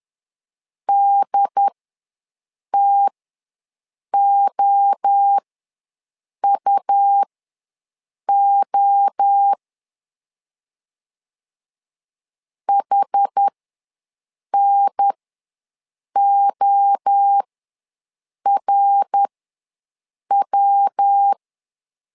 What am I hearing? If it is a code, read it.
Morse code, heard as a keyed tone, about 10 wpm, spaced as for 6 wpm, 800 Hz. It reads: DTOUO HNORW
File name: LEZIONI / 3 MWRO / 3 MWRO ws 10wpm 2310